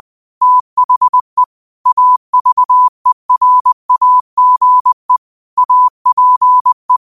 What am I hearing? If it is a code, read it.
Morse code, heard as a keyed tone, about 20 wpm, 1000 Hz. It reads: THE AVERAGE APE
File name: qotdfiles / 20wpm